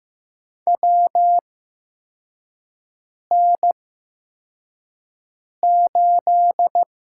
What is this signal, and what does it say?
Morse code, heard as a keyed tone, about 15 wpm, spaced as for 4 wpm, 700 Hz. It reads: WN8
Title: USA Call Signs - 8x Character Spacing - 15wpm